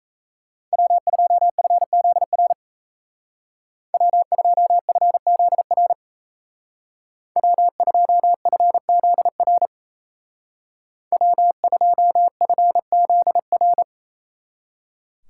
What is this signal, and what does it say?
Morse code, heard as a keyed tone, about 34 wpm, 700 Hz. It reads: W2FZR W2FZR W2FZR W2FZR
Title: Speed Racing - USA 1x3 Call Signs 28wpm